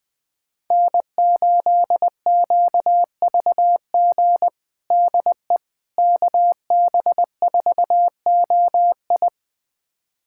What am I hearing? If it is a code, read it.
Morse code, heard as a keyed tone, about 20 wpm, 700 Hz. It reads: N8QVG DE KB4OI
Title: USA Callsign DE Callsign - 5 Characters - 20wpm